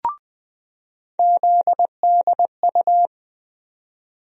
Morse code, heard as a keyed tone, about 21 wpm, 700 Hz. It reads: ZDU
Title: USA Call Sign Postfixes - 3 Letters 20wpm